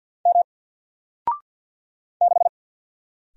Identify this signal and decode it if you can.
Morse code, heard as a keyed tone, about 50 wpm, 700 Hz. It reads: M 6
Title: Single Letter-Number Alt 50wpm